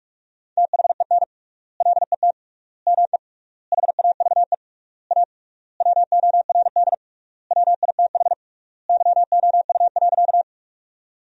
Morse code, heard as a keyed tone, about 46 wpm, 700 Hz. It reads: THEN LET ME HAVE A WORD WITH YOU.
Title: 3x Word-Spacing - Sentences from Top 300 Words - 45wpm